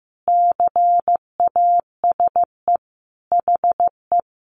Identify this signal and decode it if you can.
Morse code, heard as a keyed tone, about 15 wpm, 700 Hz. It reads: CASE HE
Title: Sentences from Top 1300 Words 15wpm